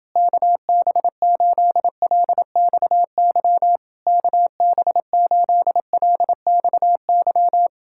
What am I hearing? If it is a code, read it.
Morse code, heard as a keyed tone, about 29 wpm, 700 Hz. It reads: KB8LXY KB8LXY